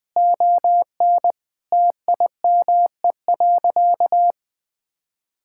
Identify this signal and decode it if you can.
Morse code, heard as a keyed tone, about 20 wpm, 700 Hz. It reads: ON TIME.
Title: Sentences with 3-Word Phrases - 20wpm